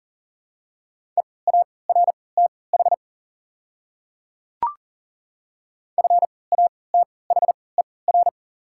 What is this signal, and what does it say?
Morse code, heard as a keyed tone, about 40 wpm, spaced as for 23 wpm, 690 Hz. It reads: EARTH FATHER
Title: 3x Character Spacing - Top 200 Words - 40wpm